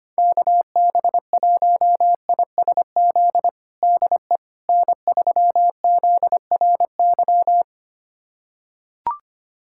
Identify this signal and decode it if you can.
Morse code, heard as a keyed tone, about 25 wpm, 700 Hz. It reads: KB1ISZ DE N3ZRY
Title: USA Callsign DE Callsign 25wpm